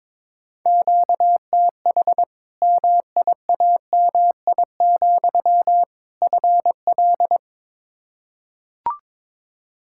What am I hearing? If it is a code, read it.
Morse code, heard as a keyed tone, about 22 wpm, 700 Hz. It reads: QTH MIAMI, FL
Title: Practice QSO - Exchange 1 - Beginning - 22wpm